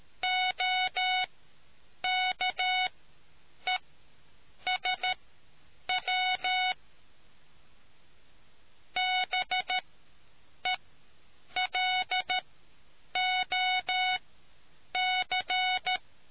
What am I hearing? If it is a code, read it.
Morse code, heard as a keyed tone, about 13 wpm, spaced as for 8 wpm, 740 Hz. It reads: OKESW BELOC